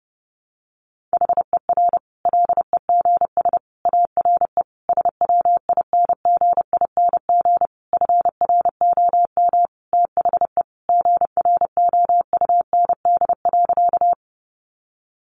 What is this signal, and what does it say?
Morse code, heard as a keyed tone, about 30 wpm, 700 Hz. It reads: HER LEGS ARE SWINGING FROM THE GROUND.
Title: Easy Phrases 30wpm